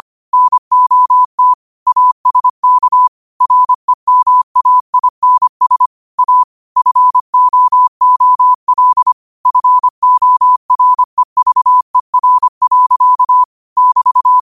Morse code, heard as a keyed tone, about 25 wpm, 1000 Hz. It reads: NOT ASK REMAINS A FOOL FOREVER. <BT>